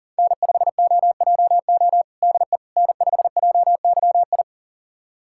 Morse code, heard as a keyed tone, about 37 wpm, 700 Hz. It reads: N5OJO DE N5JYI
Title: USA Callsign DE Callsign - 5 Characters - 40wpm